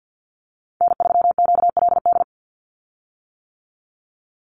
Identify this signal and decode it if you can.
Morse code, heard as a keyed tone, about 50 wpm, 700 Hz. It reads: N3QLD
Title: USA Call Signs 50wpm